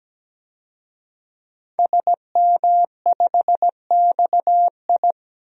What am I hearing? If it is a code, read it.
Morse code, heard as a keyed tone, about 17 wpm, 700 Hz. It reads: SM5XI
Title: DXCC-100 Call Signs - 5 Characters - Repeated 2x - 17wpm